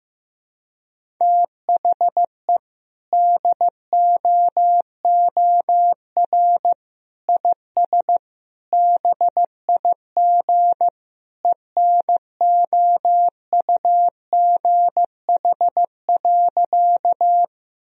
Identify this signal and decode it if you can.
Morse code, heard as a keyed tone, about 16 wpm, 700 Hz. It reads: THE DOOR IS BIG ENOUGH.
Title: Sentences from Top 1000 Words - Review Part 1 - 15wpm